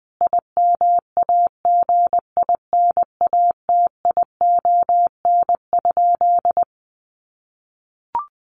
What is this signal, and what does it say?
Morse code, heard as a keyed tone, about 20 wpm, 700 Hz. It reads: IMAGINATION?
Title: Sentences from Top 1300 Words 20wpm